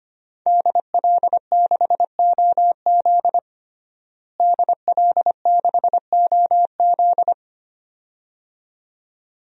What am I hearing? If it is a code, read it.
Morse code, heard as a keyed tone, about 25 wpm, 700 Hz. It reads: DL6OZ DL6OZ
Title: DXCC-100 Call Signs - 5 Characters - Repeated 2x - 25wpm